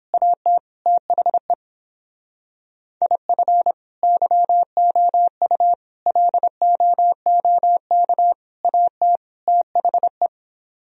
Morse code, heard as a keyed tone, about 26 wpm, 700 Hz. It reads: AT THE IF YOU LOOK AT THE
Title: Speed-Racing - Sets of 5 Words 22wpm